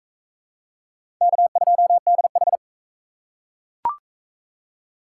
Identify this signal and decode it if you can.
Morse code, heard as a keyed tone, about 41 wpm, 700 Hz. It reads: K2DH